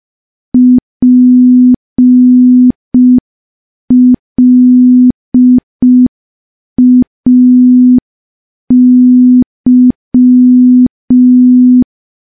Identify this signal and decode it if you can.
Morse code, heard as a keyed tone, about 5 wpm, 250 Hz. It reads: PLAY
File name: morse / morse-training-05WPM